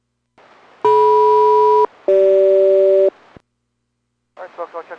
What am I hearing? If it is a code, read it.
Aircraft SELCAL, heard as two pairs of tones, with speech. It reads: DM-CG